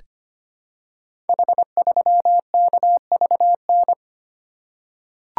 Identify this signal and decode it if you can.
Morse code, heard as a keyed tone, about 25 wpm, 700 Hz. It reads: H3KVN
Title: Code Groups - 5 Letter:Numbers - 25wpm